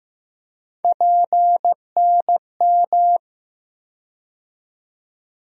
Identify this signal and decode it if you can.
Morse code, heard as a keyed tone, about 15 wpm, 700 Hz. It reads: PNM
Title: USA Call Sign Postfixes - 3 Letters 15wpm